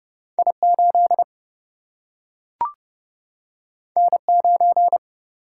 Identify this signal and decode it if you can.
Morse code, heard as a keyed tone, about 30 wpm, 700 Hz. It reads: I 8 N 9
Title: Single-Letter Single-Number 30wpm